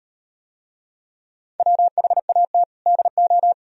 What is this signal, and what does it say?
Morse code, heard as a keyed tone, about 35 wpm, 700 Hz. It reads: WHAT DO